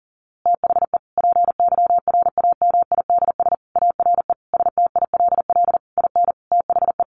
Morse code, heard as a keyed tone, about 40 wpm, 700 Hz. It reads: THE PYRAMIDS ARE STILL IN THE
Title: A Sound Of Thunder 40wpm